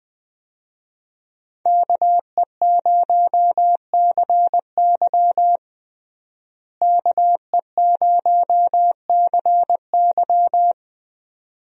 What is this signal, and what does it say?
Morse code, heard as a keyed tone, about 20 wpm, 700 Hz. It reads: KE0CY KE0CY
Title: USA Call Signs - 5 Characters - Repeated 2x - 20wpm